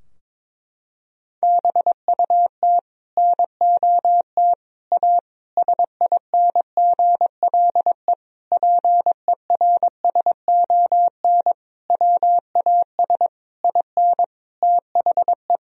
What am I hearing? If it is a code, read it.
Morse code, heard as a keyed tone, about 22 wpm, 700 Hz. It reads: BUT NOT A SINGLE PERSON WAS IN THE
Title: Sentences from Top 1000 Words - Review Part 4 - 22wpm